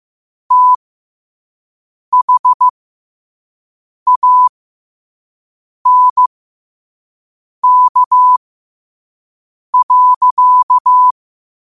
Morse code, heard as a keyed tone, about 15 wpm, spaced as for 5 wpm, 1000 Hz. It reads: THANK.
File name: qotdfiles / 5wpm